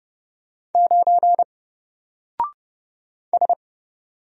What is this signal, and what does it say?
Morse code, heard as a keyed tone, about 30 wpm, 700 Hz. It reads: 9 S
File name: Single-Letter-Number-30wpm